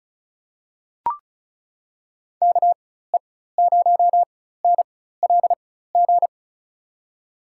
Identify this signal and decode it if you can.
Morse code, heard as a keyed tone, about 36 wpm, spaced as for 16 wpm, 700 Hz. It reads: KE0NLG
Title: USA Call Signs - 4x Character Spacing - 35wpm